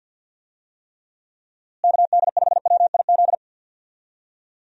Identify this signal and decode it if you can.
Morse code, heard as a keyed tone, about 51 wpm, 700 Hz. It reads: KD5WIZ
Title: USA Call Signs - 6 Characters - Repeated 2x - 50wpm